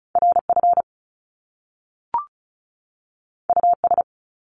Morse code, heard as a keyed tone, about 36 wpm, 700 Hz. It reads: RF US
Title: Top Two Letter Combinations in Words 35wpm